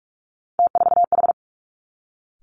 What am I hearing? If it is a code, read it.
Morse code, heard as a keyed tone, about 46 wpm, 700 Hz. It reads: T4H
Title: Three Challenging Letter-Number Combinations 45wpm